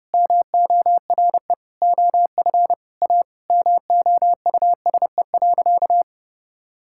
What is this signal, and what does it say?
Morse code, heard as a keyed tone, about 30 wpm, 700 Hz. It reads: MORE OF A MOUSE.
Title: Sentences from Top 1000 Words - Review Part 5 - 30wpm